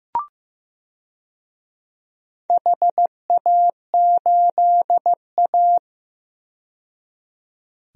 Morse code, heard as a keyed tone, about 15 wpm, 700 Hz. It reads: HA8A
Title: DXCC-100 Call Signs - 4 Characters - 15wpm